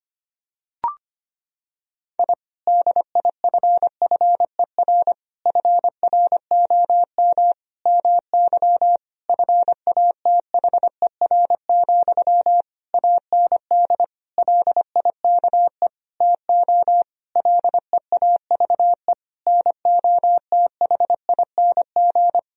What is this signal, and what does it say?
Morse code, heard as a keyed tone, about 25 wpm, 700 Hz. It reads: I DIFFER FROM MY FATHER, AND LIKE TO LEAVE NOTHING
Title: Sentences from Top 1000 Words - Review Part 1 - 25wpm